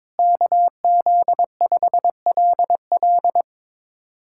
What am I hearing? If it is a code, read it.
Morse code, heard as a keyed tone, about 22 wpm, 700 Hz. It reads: KZ5LL
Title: USA Callsign DE Callsign 22wpm